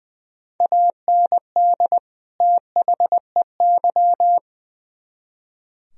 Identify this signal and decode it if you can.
Morse code, heard as a keyed tone, about 24 wpm, 700 Hz. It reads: AND THEY
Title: Speed-Racing - Sets of 2 Words 20wpm